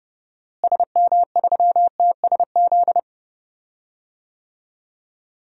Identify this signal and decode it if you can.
Morse code, heard as a keyed tone, about 31 wpm, 700 Hz. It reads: SM3TSZ
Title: DXCC-100 Call Signs - 6 Characters - Repeated 2x - 30wpm